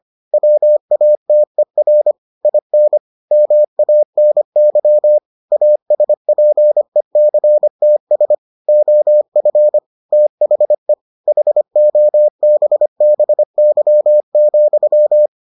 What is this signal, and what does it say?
Morse code, heard as a keyed tone, about 25 wpm, 600 Hz. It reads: WATER IN MANY ASPECTS OF THE HOBBY,